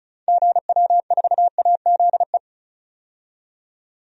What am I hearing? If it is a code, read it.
Morse code, heard as a keyed tone, about 36 wpm, 700 Hz. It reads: GW4AZE
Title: DXCC-100 Call Signs - 6 Characters - 35wpm